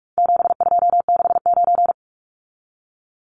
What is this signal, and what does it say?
Morse code, heard as a keyed tone, about 44 wpm, 700 Hz. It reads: 7269
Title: Numbers - 4-Digit 45wpm